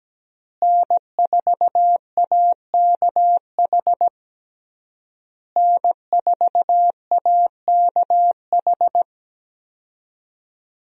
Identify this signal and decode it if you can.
Morse code, heard as a keyed tone, about 17 wpm, 700 Hz. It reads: N4AKH N4AKH
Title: USA Call Signs - 5 Characters - Repeated 3x - 17wpm